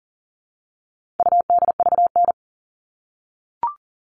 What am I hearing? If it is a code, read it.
Morse code, heard as a keyed tone, about 40 wpm, 700 Hz. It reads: UDVN
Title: All Cut Numbers - 4-Digit 40wpm